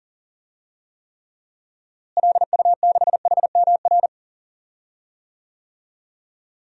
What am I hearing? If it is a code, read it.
Morse code, heard as a keyed tone, about 40 wpm, 700 Hz. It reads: LU6HMR
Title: DXCC-100 Call Signs - 6 Characters - Repeated 3x - 40wpm